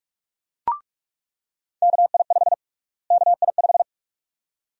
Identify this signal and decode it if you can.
Morse code, heard as a keyed tone, about 46 wpm, 700 Hz. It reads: KI5 KI5
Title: USA Call Sign Prefixes - Repeated 2x - 45wpm